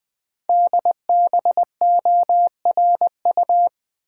Morse code, heard as a keyed tone, about 20 wpm, 700 Hz. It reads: DBORU
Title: Code Groups - 5 Letters - 20wpm